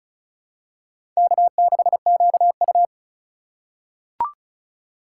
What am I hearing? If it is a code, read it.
Morse code, heard as a keyed tone, about 34 wpm, 700 Hz. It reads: K6QU
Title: USA Call Signs - 4 Characters - 4x Character Spacing - 35wpm